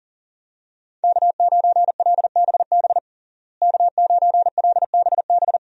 Morse code, heard as a keyed tone, about 40 wpm, 700 Hz. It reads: K9LBB K9LBB
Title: USA Call Signs - 5 Characters - Repeated 2x - 40wpm